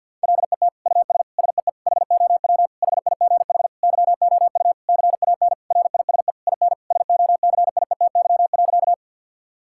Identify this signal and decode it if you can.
Morse code, heard as a keyed tone, about 50 wpm, 700 Hz. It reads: LET US SEE HOW HIGH YOU CAN RISE IN SOCIETY.
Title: Sentences from Top 1000 Words - Review Part 6 - 50wpm